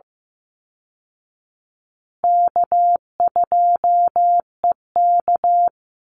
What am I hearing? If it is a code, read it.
Morse code, heard as a keyed tone, about 15 wpm, 700 Hz. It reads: K2EK
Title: Easy Call Signs Repeated 2x - 15wpm